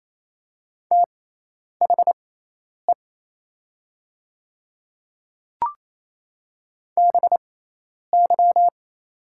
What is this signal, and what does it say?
Morse code, heard as a keyed tone, about 27 wpm, spaced as for 10 wpm, 690 Hz. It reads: THE BY
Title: 6x Character Spacing - Top 100 Words - 28wpm